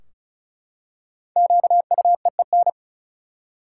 Morse code, heard as a keyed tone, about 34 wpm, 700 Hz. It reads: QUEEN
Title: Five Letter Words - 35wpm